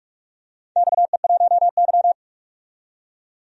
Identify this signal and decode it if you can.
Morse code, heard as a keyed tone, about 41 wpm, 700 Hz. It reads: XE1Y
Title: DXCC-100 Call Signs - 4 Characters - 45wpm